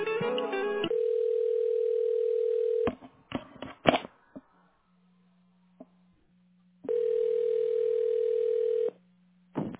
Ringback tone (North America), 440 + 480 Hz.